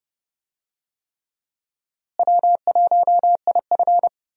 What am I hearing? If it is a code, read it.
Morse code, heard as a keyed tone, about 32 wpm, 700 Hz. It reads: W1IF
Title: USA Call Signs - 4 Characters - Repeated 3x - 30wpm